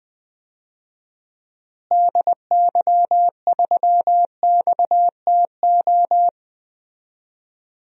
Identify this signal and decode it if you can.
Morse code, heard as a keyed tone, about 20 wpm, 700 Hz. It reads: DY3XTO
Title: DXCC-100 Call Signs - 6 Characters - Repeated 3x - 20wpm